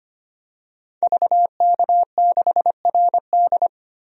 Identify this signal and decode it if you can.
Morse code, heard as a keyed tone, about 25 wpm, 700 Hz. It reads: VK6RD
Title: DXCC-100 Call Signs - 5 Characters - Repeated 3x - 25wpm